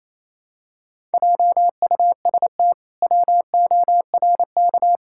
Morse code, heard as a keyed tone, about 27 wpm, 700 Hz. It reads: JUST WORK
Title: Pangram Challenge v2 - 28wpm